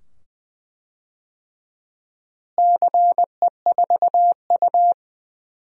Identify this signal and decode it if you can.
Morse code, heard as a keyed tone, about 20 wpm, 700 Hz. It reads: CE4U
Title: DXCC-100 Call Signs - 4 Characters - Repeated 2x - 20wpm